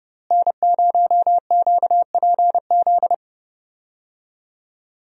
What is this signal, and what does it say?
Morse code, heard as a keyed tone, about 30 wpm, 700 Hz. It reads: N0QPZ
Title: USA Call Signs - 5 Characters - Repeated 3x - 30wpm